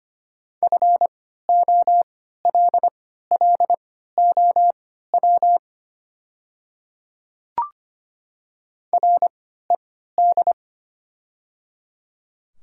Morse code, heard as a keyed tone, about 25 wpm, spaced as for 14 wpm, 700 Hz. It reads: FOLLOW RED
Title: 3x Character Spacing - Top 500 Words - 25wpm